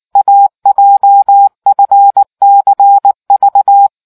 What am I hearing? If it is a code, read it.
Morse code, heard as a keyed tone, about 19 wpm, 800 Hz. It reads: AJFCV